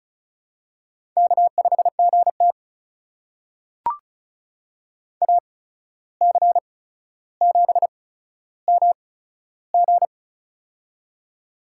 Morse code, heard as a keyed tone, about 36 wpm, spaced as for 14 wpm, 700 Hz. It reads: K5GT AC7MG
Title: USA Call Signs - 8x Character Spacing - 35wpm